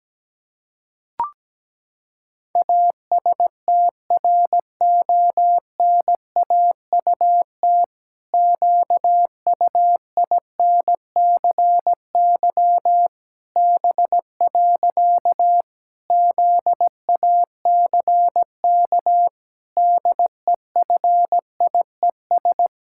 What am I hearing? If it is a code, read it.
Morse code, heard as a keyed tone, about 17 wpm, 700 Hz. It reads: ASTRONAUT QUINCY B. ZACK DEFIES
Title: Pangram Challenge v2 - 17wpm